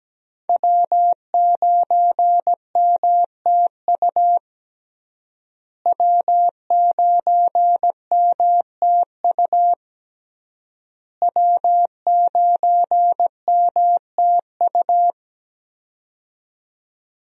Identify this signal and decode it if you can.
Morse code, heard as a keyed tone, about 17 wpm, 700 Hz. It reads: W9MTU W9MTU W9MTU
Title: USA Call Signs - 5 Characters - Repeated 3x - 17wpm